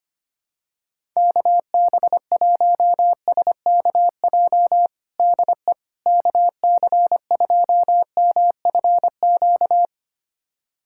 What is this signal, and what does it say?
Morse code, heard as a keyed tone, about 25 wpm, 700 Hz. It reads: KB1SKJ DE KC2MFQ
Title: USA Callsign DE Callsign 25wpm